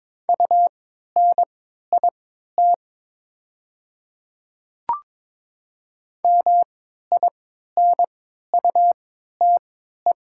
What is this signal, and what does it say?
Morse code, heard as a keyed tone, about 22 wpm, spaced as for 12 wpm, 700 Hz. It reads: UNIT MINUTE
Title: 3x Character Spacing - Top 500 Words - 22wpm